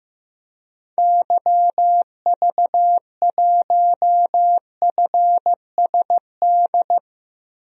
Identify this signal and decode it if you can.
Morse code, heard as a keyed tone, about 16 wpm, 700 Hz. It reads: YV1FSD
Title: DXCC-100 Call Signs - 6 Characters - Repeated 2x - 15wpm